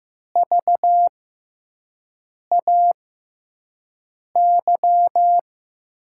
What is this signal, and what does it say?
Morse code, heard as a keyed tone, about 15 wpm, 700 Hz. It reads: V A Y
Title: USA Call Signs - 6x Character Spacing - 15wpm